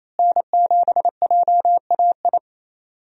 Morse code, heard as a keyed tone, about 28 wpm, 700 Hz. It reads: N7JAI
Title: USA Call Signs - 5 Characters - Repeated 3x - 28wpm